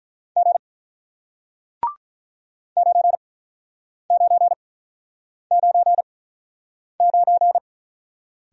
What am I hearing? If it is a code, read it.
Morse code, heard as a keyed tone, about 48 wpm, spaced as for 9 wpm, 700 Hz. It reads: G 9999